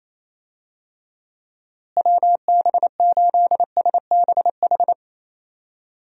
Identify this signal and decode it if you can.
Morse code, heard as a keyed tone, about 27 wpm, 700 Hz. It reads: WB8SBH
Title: USA Call Signs - 6 Characters - Repeated 3x - 28wpm